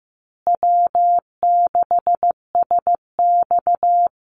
Morse code, heard as a keyed tone, about 14 wpm, 700 Hz. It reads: W6SX
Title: Call Signs Repeated 3x - 15wpm